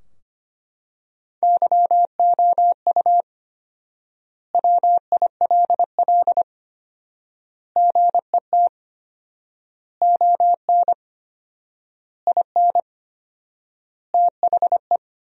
Morse code, heard as a keyed tone, about 25 wpm, 700 Hz. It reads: YOU WILL GET ON IN THE
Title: 4x Word-Spacing - Sentences from Top 200 Words - 25wpm